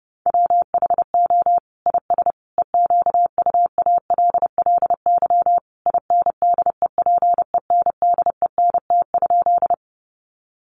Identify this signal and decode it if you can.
Morse code, heard as a keyed tone, about 30 wpm, 700 Hz. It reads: WHO IS EQUALLY INDEPENDENT?
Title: Sentences from Top 1400 Words 30wpm